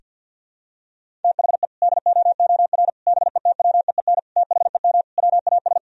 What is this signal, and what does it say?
Morse code, heard as a keyed tone, about 46 wpm, 700 Hz. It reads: THE DOOR BETWEEN THEM WAS
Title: Sentences from Top 1000 Words - Review Part 1 - 50wpm